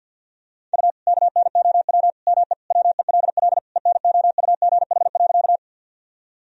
Morse code, heard as a keyed tone, about 52 wpm, 700 Hz. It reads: U KNOW ME WELL ENOUGH.
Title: Sentences from Top 400 Words and Abbreviations 50wpm